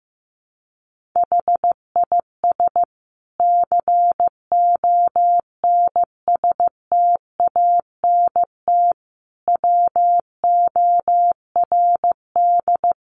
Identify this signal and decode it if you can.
Morse code, heard as a keyed tone, about 15 wpm, 700 Hz. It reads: HIS CONSTANT WORD
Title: Sentences from Top 1500 Words 15wpm